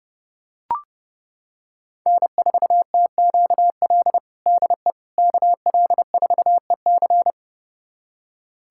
Morse code, heard as a keyed tone, about 30 wpm, 700 Hz. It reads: N4TQL DE KL4EC
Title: USA Callsign DE Callsign - 5 Characters - 30wpm